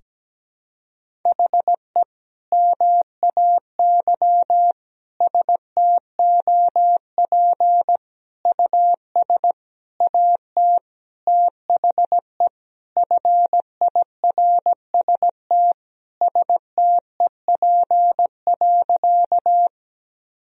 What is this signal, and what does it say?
Morse code, heard as a keyed tone, about 17 wpm, 700 Hz. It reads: HE MAY STOP US AT THE FIRST STEP.
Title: Sentences from Top 1000 Words - Review Part 2 - 17wpm